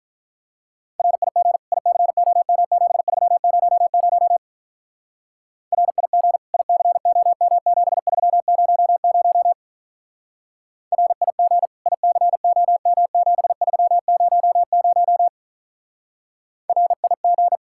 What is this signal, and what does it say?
Morse code, heard as a keyed tone, about 44 wpm, 700 Hz. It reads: RIG ICOM7300 RIG ICOM7300 RIG ICOM7300 RIG